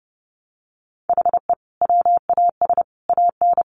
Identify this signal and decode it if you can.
Morse code, heard as a keyed tone, about 30 wpm, 700 Hz. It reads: HE WAS AN
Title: Sentences from Top 1200 Words 30wpm